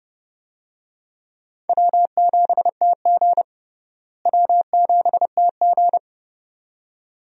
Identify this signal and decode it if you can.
Morse code, heard as a keyed tone, about 30 wpm, 700 Hz. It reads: W7TG W7TG
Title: USA Call Signs - 4 Characters - Repeated 2x - 30wpm